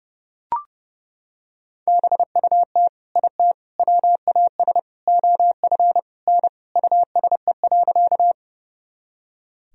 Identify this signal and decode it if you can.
Morse code, heard as a keyed tone, about 30 wpm, 700 Hz. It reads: BUT IT WAS OF N USE.
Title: Sentences from Top 100 Words and Abbreviations 30wpm